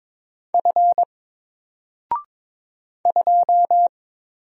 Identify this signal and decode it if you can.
Morse code, heard as a keyed tone, about 22 wpm, 700 Hz. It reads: F 2
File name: Single-Letter-Number-22wpm